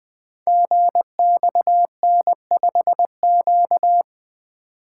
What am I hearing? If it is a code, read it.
Morse code, heard as a keyed tone, about 20 wpm, 700 Hz. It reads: GXN5Q
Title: Code Groups - 5 Letter:Numbers - 20wpm